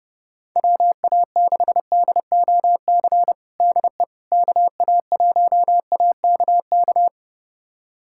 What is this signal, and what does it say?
Morse code, heard as a keyed tone, about 30 wpm, 700 Hz. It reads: WA6DOC DE KA1AKK
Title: USA Callsign DE Callsign - 6 Characters - 30wpm